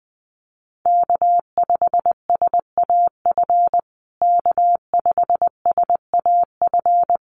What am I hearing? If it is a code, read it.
Morse code, heard as a keyed tone, about 19 wpm, 700 Hz. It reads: K5SAF K5SAF
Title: USA Call Signs Repeated 2x 20wpm